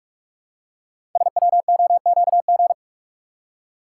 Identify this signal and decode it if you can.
Morse code, heard as a keyed tone, about 44 wpm, 700 Hz. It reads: SWOQG
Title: Code Groups - 5 Letters - 45wpm